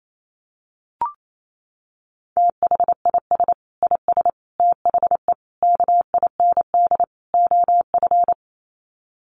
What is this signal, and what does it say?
Morse code, heard as a keyed tone, about 28 wpm, 700 Hz. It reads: THIS IS THE KIND OF
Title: Sets of 5 Words Encore 28wpm